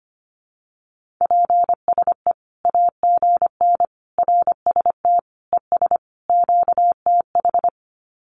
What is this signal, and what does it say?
Morse code, heard as a keyed tone, about 25 wpm, 700 Hz. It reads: PSE AGN RST ES QTH
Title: Common QSO Phrases 25wpm